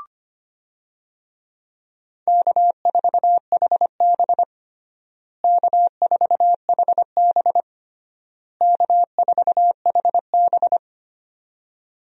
Morse code, heard as a keyed tone, about 25 wpm, 700 Hz. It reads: K4HB K4HB K4HB
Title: USA Call Signs - 4 Characters - Repeated 3x - 25wpm